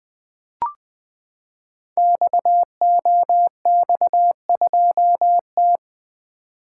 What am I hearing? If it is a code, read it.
Morse code, heard as a keyed tone, about 20 wpm, 700 Hz. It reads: XOX2T
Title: Code Groups - 5 Letter:Numbers - 20wpm